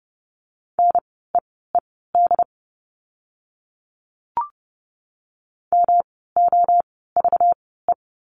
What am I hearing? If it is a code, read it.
Morse code, heard as a keyed tone, about 30 wpm, spaced as for 17 wpm, 700 Hz. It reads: NEED MOVE
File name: course / 30:10wpm / 237 Lesson 38f - Letter q - Words 30wpm